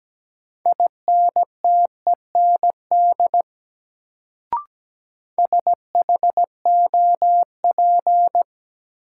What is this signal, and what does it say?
Morse code, heard as a keyed tone, about 17 wpm, 700 Hz. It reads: INTEND SHOP
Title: Sentences from Top 1600 Words - New Word Drill - 17wpm